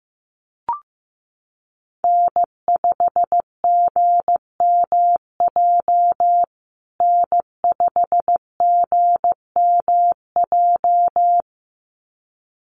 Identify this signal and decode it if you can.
Morse code, heard as a keyed tone, about 15 wpm, 700 Hz. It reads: N5GMJ N5GMJ
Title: USA Call Signs Repeated 2x 15wpm